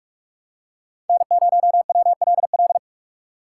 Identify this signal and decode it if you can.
Morse code, heard as a keyed tone, about 46 wpm, 700 Hz. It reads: N0WLL
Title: USA Call Signs - 5 Characters - 4x Character Spacing - 45wpm